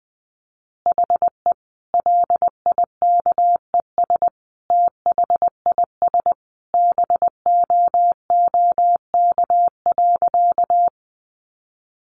Morse code, heard as a keyed tone, about 20 wpm, 700 Hz. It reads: HE LIKES THIS BOOK.
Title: Additional Easy Phrases 20wpm